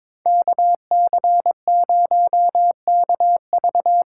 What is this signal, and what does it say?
Morse code, heard as a keyed tone, about 24 wpm, 700 Hz. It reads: KC0KV
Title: Speed Racing - USA 2x2 Call Signs 22wpm